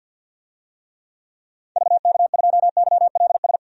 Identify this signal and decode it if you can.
Morse code, heard as a keyed tone, about 50 wpm, 700 Hz. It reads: VK2YLS
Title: DXCC-100 Call Signs - 6 Characters - 50wpm